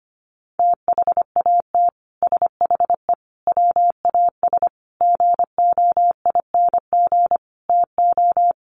Morse code, heard as a keyed tone, about 25 wpm, 700 Hz. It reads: THAT SHE WAS GOING TO
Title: Sets of 5 Words Encore2 25wpm